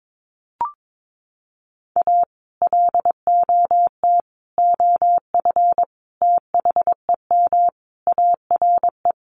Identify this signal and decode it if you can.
Morse code, heard as a keyed tone, about 22 wpm, 700 Hz. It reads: A LOT OF THEM ARE
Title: Sets of 5 Words Encore2 22wpm